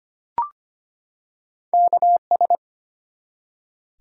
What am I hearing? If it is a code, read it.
Morse code, heard as a keyed tone, about 25 wpm, 700 Hz. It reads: KS
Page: morse-code-ninja-practice-25wpm